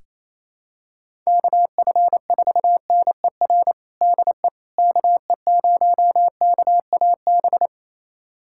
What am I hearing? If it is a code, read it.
Morse code, heard as a keyed tone, about 28 wpm, 700 Hz. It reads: KF4NER DE KE0KAB
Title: USA Callsign DE Callsign 28wpm